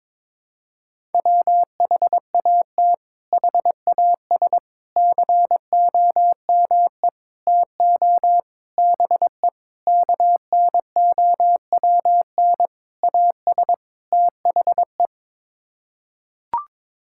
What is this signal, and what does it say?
Morse code, heard as a keyed tone, about 22 wpm, 700 Hz. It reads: WHAT HAS COME TO BE KNOWN AS THE
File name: practice / 22wpm / Sets of 8 Words - 22wpm